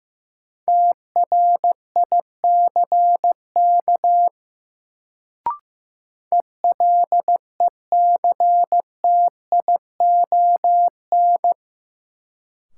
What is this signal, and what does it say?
Morse code, heard as a keyed tone, about 15 wpm, 700 Hz. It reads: TRICK ELECTION
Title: Sentences from Top 1500 Words - New Word Drill - 15wpm